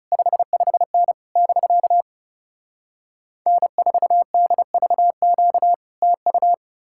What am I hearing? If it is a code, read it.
Morse code, heard as a keyed tone, about 40 wpm, 700 Hz. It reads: 55N <BK> N4DVQ TU